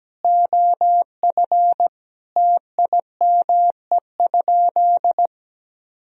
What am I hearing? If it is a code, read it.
Morse code, heard as a keyed tone, about 17 wpm, 700 Hz. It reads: OF TIME?
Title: Sentences from Top 1000 Words - Review Part 3 - 17wpm